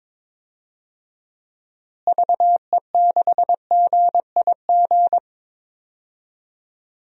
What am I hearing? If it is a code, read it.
Morse code, heard as a keyed tone, about 21 wpm, 700 Hz. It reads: VE6GIG
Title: DXCC-100 Call Signs - 6 Characters - Repeated 3x - 22wpm